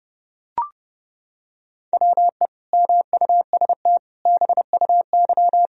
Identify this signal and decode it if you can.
Morse code, heard as a keyed tone, about 28 wpm, 700 Hz. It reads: WE MUST BUY
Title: Sentences from Top 1000 Words - Review Part 4 - 30wpm